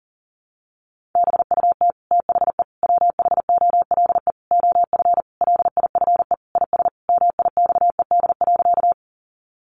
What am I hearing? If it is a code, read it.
Morse code, heard as a keyed tone, about 40 wpm, 700 Hz. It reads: BUT THE WHOLE OF LIFE IS MIXED.
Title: Sentences from Top 1500 Words 40wpm